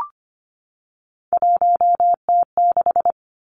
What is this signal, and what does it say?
Morse code, heard as a keyed tone, about 25 wpm, 700 Hz. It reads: 1T6
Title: Three Challenging Letter-Number Combinations 25wpm